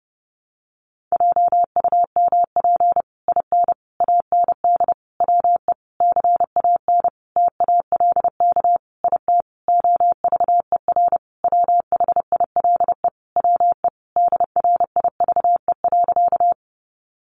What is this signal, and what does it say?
Morse code, heard as a keyed tone, about 30 wpm, 700 Hz. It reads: JUMP IN AND WE CAN TALK IT OVER WHILE WE DRIVE.
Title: Sentences from Top 1300 Words 30wpm